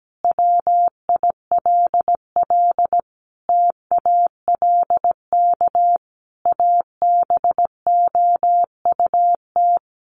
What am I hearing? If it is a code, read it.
Morse code, heard as a keyed tone, about 17 wpm, 700 Hz. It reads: WILL TALK ABOUT
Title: Sentences from Top 1200 Words 17wpm